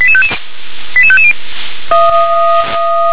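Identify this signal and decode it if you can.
Five-tone selective call (ZVEI), heard as quick runs of five tones, tones of about 70 ms. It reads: 88550
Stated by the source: Five-tone ZVEI selective call with code 88839, received by SDR.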